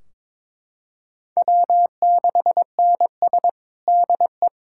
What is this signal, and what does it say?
Morse code, heard as a keyed tone, about 22 wpm, 700 Hz. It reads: W6NS DE
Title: Practice QSO - Exchange 4 - Ending - 22wpm